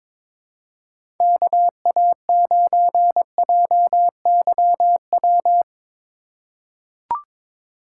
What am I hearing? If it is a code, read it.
Morse code, heard as a keyed tone, about 22 wpm, 700 Hz. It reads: KA9JYW
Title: USA Call Signs - 3x Character Spacing - 22wpm